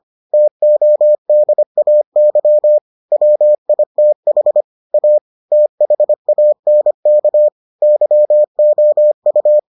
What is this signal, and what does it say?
Morse code, heard as a keyed tone, about 25 wpm, 600 Hz. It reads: TODAY WITH A THANK YOU